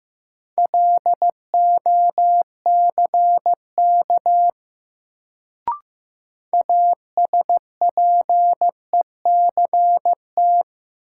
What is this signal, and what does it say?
Morse code, heard as a keyed tone, about 15 wpm, 700 Hz. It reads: LOCK ASPECT
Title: Sentences from Top 1300 Words - New Word Drill Alt - 15wpm